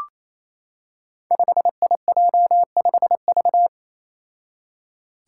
Morse code, heard as a keyed tone, about 29 wpm, 700 Hz. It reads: 5IJ5V
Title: Five Challenging Letter-Number Combinations 28wpm